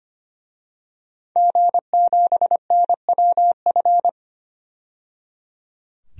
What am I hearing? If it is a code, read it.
Morse code, heard as a keyed tone, about 25 wpm, 700 Hz. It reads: G7NWF
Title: DXCC-100 Call Signs - 5 Characters - 25wpm